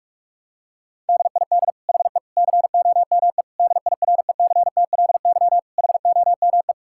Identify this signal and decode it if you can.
Morse code, heard as a keyed tone, about 47 wpm, 700 Hz. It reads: DID HE COME DIRECTLY HOME
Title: Sentences from Top 1000 Words - Review Part 5 - 45wpm